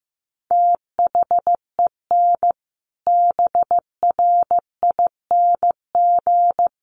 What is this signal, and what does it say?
Morse code, heard as a keyed tone, about 15 wpm, 700 Hz. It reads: THEN BRING
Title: Sentences from Top 500 Words 15wpm